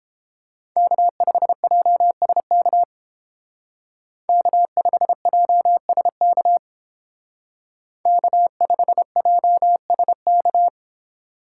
Morse code, heard as a keyed tone, about 30 wpm, 700 Hz. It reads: K5JSK K5JSK K5JSK